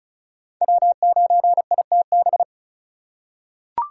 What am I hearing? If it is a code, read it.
Morse code, heard as a keyed tone, about 36 wpm, 700 Hz. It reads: W9ITB
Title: USA Call Signs - 5 Characters - 3x Character Spacing - 35wpm